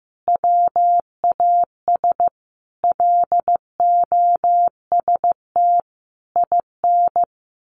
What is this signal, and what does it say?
Morse code, heard as a keyed tone, about 15 wpm, 700 Hz. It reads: WAS LOST IN
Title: A Sound Of Thunder 15wpm